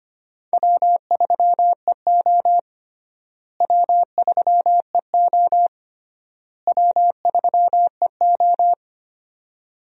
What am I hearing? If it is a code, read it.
Morse code, heard as a keyed tone, about 25 wpm, 700 Hz. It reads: W3EO W3EO W3EO
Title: USA Call Signs - 4 Characters - Repeated 3x - 25wpm